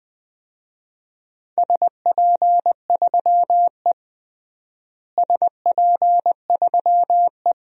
Morse code, heard as a keyed tone, about 21 wpm, 700 Hz. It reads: SP3E SP3E
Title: DXCC-100 Call Signs - 4 Characters - Repeated 2x - 20wpm